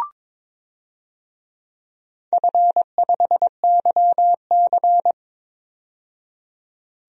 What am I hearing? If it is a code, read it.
Morse code, heard as a keyed tone, about 22 wpm, 700 Hz. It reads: F5YC
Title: DXCC-100 Call Signs - 4 Characters - 22wpm